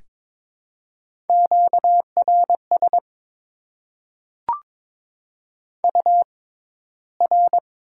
Morse code, heard as a keyed tone, about 22 wpm, spaced as for 13 wpm, 700 Hz. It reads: QRS UR
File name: Top 90 QSO Elements - 6x Character Spacing - 22wpm